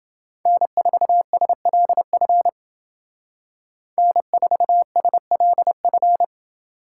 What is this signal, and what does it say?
Morse code, heard as a keyed tone, about 28 wpm, 700 Hz. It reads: N4SLF N4SLF